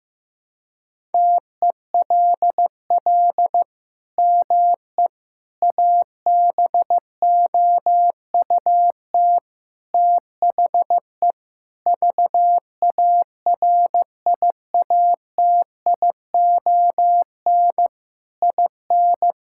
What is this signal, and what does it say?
Morse code, heard as a keyed tone, about 16 wpm, 700 Hz. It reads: TELL ME ABOUT THE VARIATION IN
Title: Sentences from Top 1600 Words 15wpm